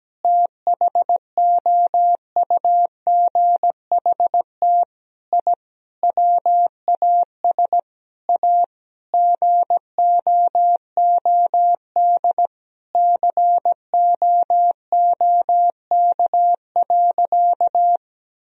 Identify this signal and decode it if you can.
Morse code, heard as a keyed tone, about 17 wpm, 700 Hz. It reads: THOUGHT I WAS A GOOD COOK.
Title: Sentences from Top 1000 Words - Review Part 4 - 17wpm